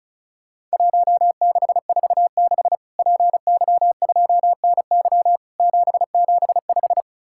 Morse code, heard as a keyed tone, about 36 wpm, 700 Hz. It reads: 1646 PY2NY 775
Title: Worked All Europe DX Contest - QTC Exchange - 35wpm